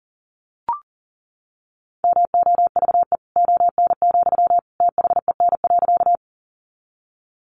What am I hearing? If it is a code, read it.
Morse code, heard as a keyed tone, about 40 wpm, 700 Hz. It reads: MOVE ON, THEN.
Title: Sentences from Top 200 Words 40wpm